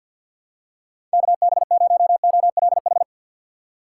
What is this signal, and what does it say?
Morse code, heard as a keyed tone, about 52 wpm, 700 Hz. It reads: KB0OLH
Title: USA Call Signs - 4x Character Spacing - 50wpm